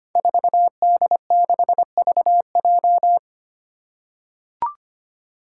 Morse code, heard as a keyed tone, about 25 wpm, 700 Hz. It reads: 4D6VJ
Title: Five Challenging Letter-Number Combinations 25wpm